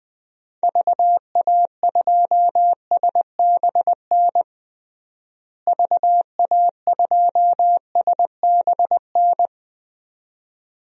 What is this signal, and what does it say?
Morse code, heard as a keyed tone, about 20 wpm, 690 Hz. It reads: VA2SBN VA2SBN
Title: DXCC-100 Call Signs - 6 Characters - Repeated 2x - 20wpm